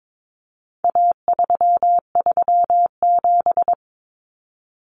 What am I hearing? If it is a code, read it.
Morse code, heard as a keyed tone, about 22 wpm, 700 Hz. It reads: A337
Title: Common Cut Numbers - 4-Digit 22wpm